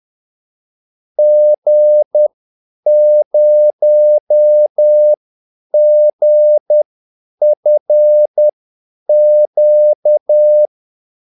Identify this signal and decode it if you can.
Morse code, heard as a keyed tone, about 10 wpm, spaced as for 8 wpm, 600 Hz. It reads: G0GFQ
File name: Callsigns-10-8wpm